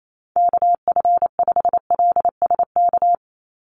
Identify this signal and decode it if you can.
Morse code, heard as a keyed tone, about 28 wpm, 700 Hz. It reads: KF5LSK
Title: USA Call Signs 28wpm